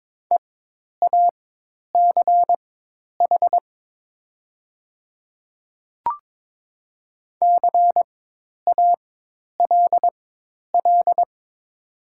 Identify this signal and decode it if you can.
Morse code, heard as a keyed tone, about 22 wpm, spaced as for 10 wpm, 700 Hz. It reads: EACH CALL
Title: 4x Character Spacing - Top 200 Words - 22wpm